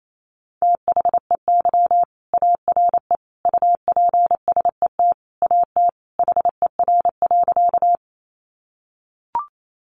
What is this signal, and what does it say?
Morse code, heard as a keyed tone, about 28 wpm, 700 Hz. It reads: THEY ARE UPSET AT HER.
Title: Sentences from Top 800 Words 28wpm